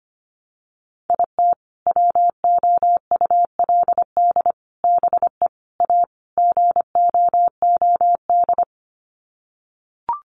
Morse code, heard as a keyed tone, about 25 wpm, 700 Hz. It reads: IT WOULD BE A GOOD
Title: Sets of 5 Words Encore2 25wpm